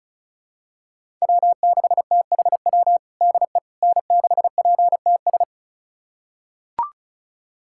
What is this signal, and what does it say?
Morse code, heard as a keyed tone, about 34 wpm, 700 Hz. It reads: W6THW DE N6PTS
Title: USA Callsign DE Callsign - 5 Characters - 35wpm